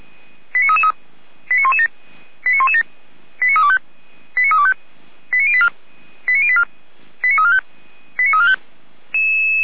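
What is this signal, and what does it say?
Five-tone selective call (ZVEI), heard as quick runs of five tones, tones of about 70 ms. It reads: 89333 89117 89117 89325 89325 89084 89084 89345 89345
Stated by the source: ZVEI five-tone selective call received by SDR, Feuerwehr Orlamünde.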